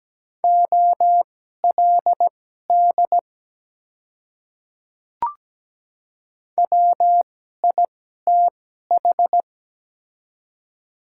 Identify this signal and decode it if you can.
Morse code, heard as a keyed tone, about 17 wpm, spaced as for 12 wpm, 700 Hz. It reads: OLD WITH
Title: 2x Character Spacing - Top 300 Words - 17wpm